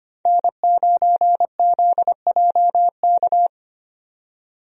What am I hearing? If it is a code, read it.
Morse code, heard as a keyed tone, about 25 wpm, 700 Hz. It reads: N9ZJK
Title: USA Call Signs - 5 Characters - 25wpm